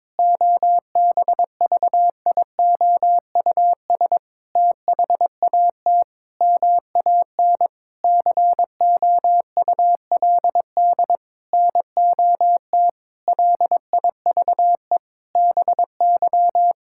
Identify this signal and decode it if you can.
Morse code, heard as a keyed tone, about 22 wpm, 700 Hz. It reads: OBVIOUS THAT MAN COULD NOT LIVE BY
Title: Sentences from Top 1600 Words 22wpm